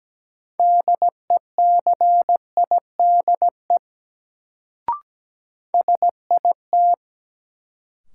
Morse code, heard as a keyed tone, about 17 wpm, 700 Hz. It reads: DECIDE SIT
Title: Sentences from Top 500 Words - New Word Drill - 17wpm